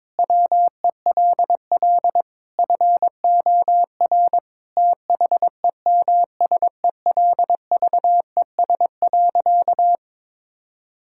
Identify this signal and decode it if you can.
Morse code, heard as a keyed tone, about 22 wpm, 700 Hz. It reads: WELL FOR THEMSELVES.
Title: Sentences from Top 1000 Words - Review Part 5 - 22wpm